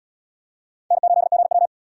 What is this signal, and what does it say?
Morse code, heard as a keyed tone, about 77 wpm, 700 Hz. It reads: N8GW